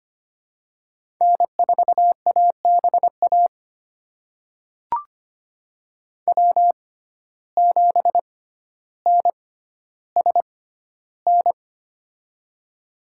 Morse code, heard as a keyed tone, about 25 wpm, spaced as for 13 wpm, 700 Hz. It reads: N4ABA W7NSN
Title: USA Call Signs - 5 Characters - 6x Character Spacing - 25wpm